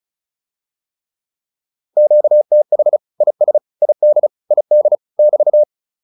Morse code, heard as a keyed tone, about 35 wpm, 600 Hz. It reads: QTH IS ID ID <BT>
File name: QSO105 35